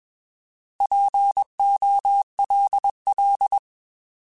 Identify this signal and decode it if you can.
Morse code, heard as a keyed tone, about 21 wpm, 800 Hz. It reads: POLL
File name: Alphabetical / Alphabetical mp3s / Ps @20